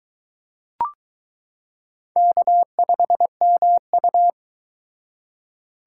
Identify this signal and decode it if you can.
Morse code, heard as a keyed tone, about 22 wpm, 700 Hz. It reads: K5MU